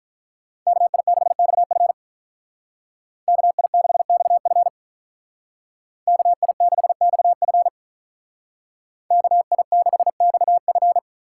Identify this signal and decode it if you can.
Morse code, heard as a keyed tone, about 46 wpm, 700 Hz. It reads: KI6XF KI6XF KI6XF KI6XF